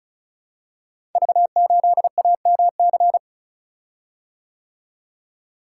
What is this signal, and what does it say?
Morse code, heard as a keyed tone, about 36 wpm, 690 Hz. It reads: V8AMC
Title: Code Groups - 5 Letter:Numbers - 35wpm